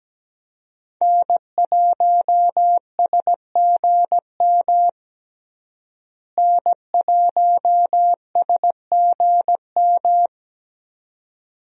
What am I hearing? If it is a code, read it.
Morse code, heard as a keyed tone, about 17 wpm, 700 Hz. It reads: N1SGM N1SGM
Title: USA Call Signs - 5 Characters - Repeated 2x - 17wpm